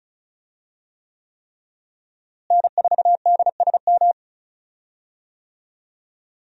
Morse code, heard as a keyed tone, about 34 wpm, 700 Hz. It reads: N4DSM
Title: USA Call Signs - 5 Characters - Repeated 3x - 35wpm